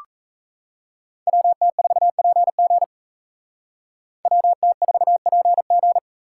Morse code, heard as a keyed tone, about 36 wpm, 700 Hz. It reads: WT4PG WT4PG